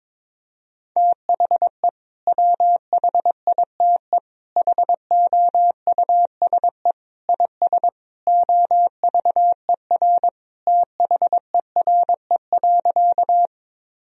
Morse code, heard as a keyed tone, about 22 wpm, 700 Hz. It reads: THE WHITE HOUSE IS OVER THERE.
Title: Sentences from Top 1000 Words - Review Part 1 - 22wpm